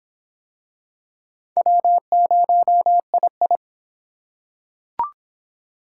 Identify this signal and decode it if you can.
Morse code, heard as a keyed tone, about 23 wpm, 700 Hz. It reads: W0II